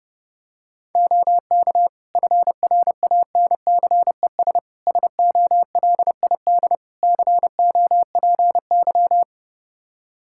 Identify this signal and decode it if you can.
Morse code, heard as a keyed tone, about 30 wpm, 700 Hz. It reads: OK FRANCES SOLID COPY
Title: Practice QSO - Exchange 3 - More Info - 30wpm